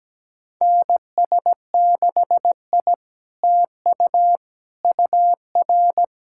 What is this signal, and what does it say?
Morse code, heard as a keyed tone, about 17 wpm, 690 Hz. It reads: NS6I TU UR
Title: SOTA QSO Practice 17wpm